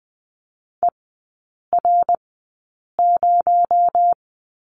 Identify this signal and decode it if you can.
Morse code, heard as a keyed tone, about 20 wpm, spaced as for 8 wpm, 700 Hz. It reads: ER0
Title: 218 Lesson 35e - Number 0 - Three Characters 20wpm